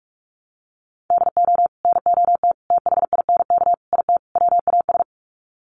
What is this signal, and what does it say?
Morse code, heard as a keyed tone, about 44 wpm, 700 Hz. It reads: DO NOT THINK IT WAS
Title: Sets of 5 Words Encore 45wpm